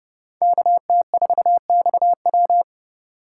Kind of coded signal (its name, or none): Morse code